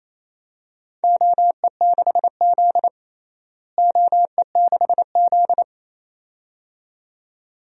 Morse code, heard as a keyed tone, about 28 wpm, 700 Hz. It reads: OE6Z OE6Z
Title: DXCC-100 Call Signs - 4 Characters - Repeated 2x - 28wpm